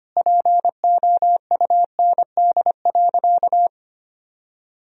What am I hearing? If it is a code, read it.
Morse code, heard as a keyed tone, about 25 wpm, 700 Hz. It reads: POUND.
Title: Sentences from Top 1000 Words - Review Part 2 - 25wpm